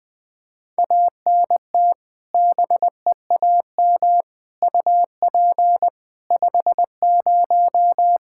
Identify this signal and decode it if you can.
Morse code, heard as a keyed tone, about 20 wpm, 700 Hz. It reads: ANT BEAM UP 50
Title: Practice QSO - Exchange 2 - Info - 20wpm